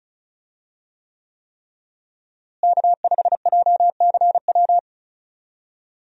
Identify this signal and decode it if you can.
Morse code, heard as a keyed tone, about 36 wpm, 700 Hz. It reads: K5JCW
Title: USA Call Signs - 5 Characters - Repeated 3x - 35wpm